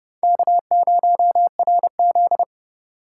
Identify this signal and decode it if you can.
Morse code, heard as a keyed tone, about 38 wpm, 700 Hz. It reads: K0RZ
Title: Speed Racing - USA 1x2 Call Signs 30wpm